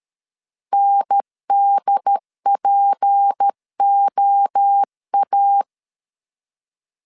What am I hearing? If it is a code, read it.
Morse code, heard as a keyed tone, about 13 wpm, 800 Hz. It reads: NDPOA
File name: LEZIONI / 6 PKX0 / 6 P 12wpm 2033